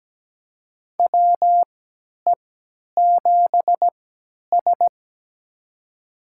Morse code, heard as a keyed tone, about 17 wpm, spaced as for 10 wpm, 700 Hz. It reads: WE7S
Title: USA Call Signs - 4 Characters - 3x Character Spacing - 17wpm